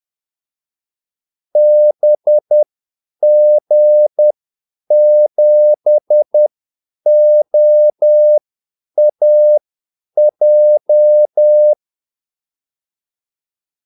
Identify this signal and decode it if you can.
Morse code, heard as a keyed tone, about 10 wpm, spaced as for 8 wpm, 600 Hz. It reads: BG7OAJ
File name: Callsigns-10-8wpm